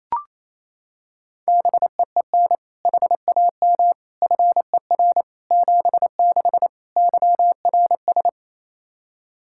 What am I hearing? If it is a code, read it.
Morse code, heard as a keyed tone, about 28 wpm, 700 Hz. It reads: BEEN HAM FER 76 YRS
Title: Practice QSO - Exchange 3 - More Info - 28wpm